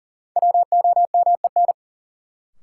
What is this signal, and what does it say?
Morse code, heard as a keyed tone, about 40 wpm, 700 Hz. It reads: WOMEN
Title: Sets of 3 Words - New Word Drill - 40wpm